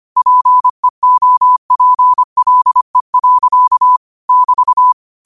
Morse code, heard as a keyed tone, about 25 wpm, 1000 Hz. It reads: PEOPLE. <BT>